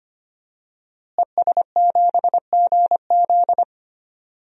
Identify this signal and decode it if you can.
Morse code, heard as a keyed tone, about 25 wpm, 700 Hz. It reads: ES7GZ
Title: DXCC-100 Call Signs - 5 Characters - Repeated 2x - 25wpm